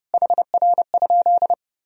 Morse code, heard as a keyed tone, about 30 wpm, 700 Hz. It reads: HR?
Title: Sentences from Top 200 Words and Abbreviations 30wpm